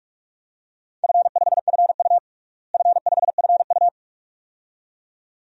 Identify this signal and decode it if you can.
Morse code, heard as a keyed tone, about 42 wpm, 700 Hz. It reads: F5FU F5FU
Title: DXCC-100 Call Signs - 4 Characters - Repeated 2x - 45wpm